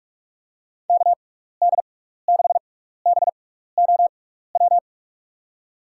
Morse code, heard as a keyed tone, about 44 wpm, spaced as for 15 wpm, 700 Hz. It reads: KD6BOW
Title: USA Call Signs - 6 Characters - 6x Character Spacing - 45wpm